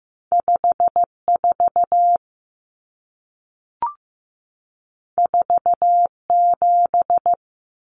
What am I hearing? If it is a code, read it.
Morse code, heard as a keyed tone, about 15 wpm, 700 Hz. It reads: 54 47
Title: Numbers - 2-Digit 15wpm